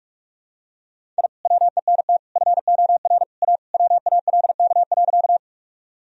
Morse code, heard as a keyed tone, about 48 wpm, 700 Hz. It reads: I WENT FOR A WALK.
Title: Sentences from Top 1000 Words - Review Part 1 - 45wpm